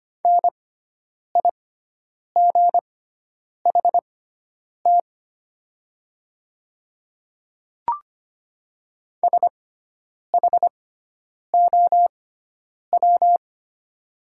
Morse code, heard as a keyed tone, about 25 wpm, spaced as for 9 wpm, 700 Hz. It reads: NIGHT SHOW
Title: 6x Character Spacing - Top 300 Words - 25wpm